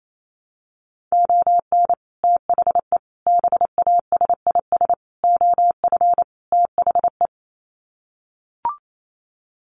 Morse code, heard as a keyed tone, about 28 wpm, 700 Hz. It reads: ON THE BASIS OF THE
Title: Sets of 5 Words 28wpm